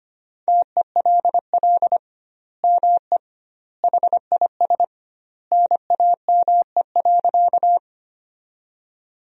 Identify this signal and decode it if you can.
Morse code, heard as a keyed tone, about 25 wpm, 700 Hz. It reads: TELL ME HIS NAME.
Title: 2x Word-Spacing - Sentences from Top 200 Words - 25wpm